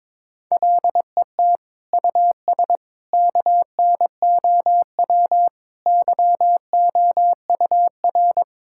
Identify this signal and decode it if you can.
Morse code, heard as a keyed tone, about 22 wpm, 700 Hz. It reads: LET US KNOW YOUR
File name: practice / 22wpm / Sets of 8 Words - 22wpm